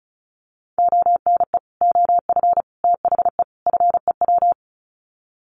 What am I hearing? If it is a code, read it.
Morse code, heard as a keyed tone, about 34 wpm, 700 Hz. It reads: ONE OF THE FEW
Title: Sets of 4 Words 35wpm